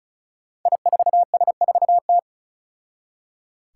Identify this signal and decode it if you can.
Morse code, heard as a keyed tone, about 36 wpm, 700 Hz. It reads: I4S4T
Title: Five Challenging Letter-Number Combinations 35wpm